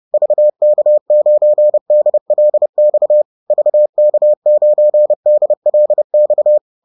Morse code, heard as a keyed tone, about 30 wpm, 600 Hz. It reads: VK9DLX VK9DLX